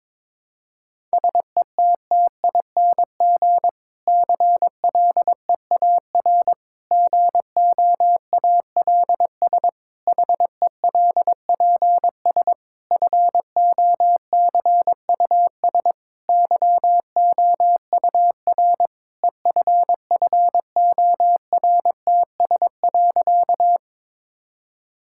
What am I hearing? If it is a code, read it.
Morse code, heard as a keyed tone, about 22 wpm, 700 Hz. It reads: SETTING CLEAR GOALS HELPS FOCUS YOUR EFFORTS.
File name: Difficult Sentences with Hints - 22wpm